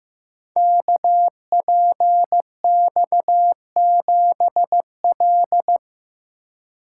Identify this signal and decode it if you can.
Morse code, heard as a keyed tone, about 15 wpm, 700 Hz. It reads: KPX7L
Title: Code Groups - 5 Letter:Numbers - 15wpm